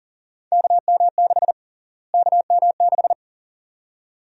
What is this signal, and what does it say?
Morse code, heard as a keyed tone, about 40 wpm, 700 Hz. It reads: KM6 KM6
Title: USA Call Sign Prefixes - Repeated 2x - 40wpm